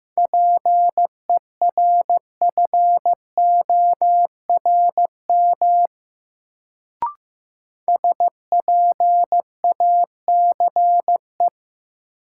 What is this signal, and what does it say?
Morse code, heard as a keyed tone, about 15 wpm, 700 Hz. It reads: PERFORM SPACE
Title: Sentences from Top 1000 Words - Review Part 5 - New Word Drill - 15wpm